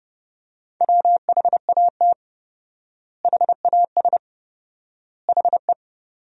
Morse code, heard as a keyed tone, about 30 wpm, 700 Hz. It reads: WHAT HAS HE
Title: 4x Word-Spacing - Sentences from Top 500 Words - 30wpm